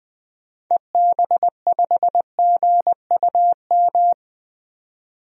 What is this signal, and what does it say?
Morse code, heard as a keyed tone, about 20 wpm, 700 Hz. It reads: EB5GUM
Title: DXCC-100 Call Signs - 6 Characters - 20wpm